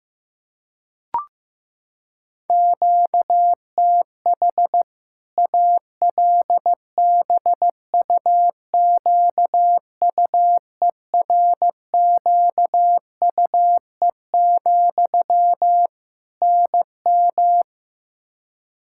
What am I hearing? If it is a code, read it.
Morse code, heard as a keyed tone, about 15 wpm, 700 Hz. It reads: QTH ALBUQUERQUE, NM
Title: Practice QSO - Exchange 1 - Beginning - 15wpm